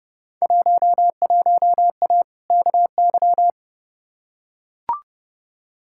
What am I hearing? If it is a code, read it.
Morse code, heard as a keyed tone, about 31 wpm, 700 Hz. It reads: 11A KY
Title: Field Day Exchanges Encore - 30wpm